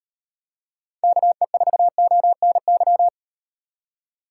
Morse code, heard as a keyed tone, about 38 wpm, 700 Hz. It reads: KE4ONY